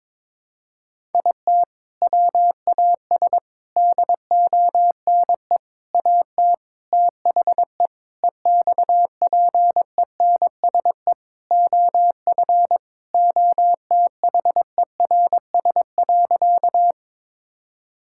Morse code, heard as a keyed tone, about 22 wpm, 700 Hz. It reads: IT WAS DONE AT THE EXPENSE OF OTHERS.
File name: practice / 22wpm / Sentences with 4-Word Phrases - 22wpm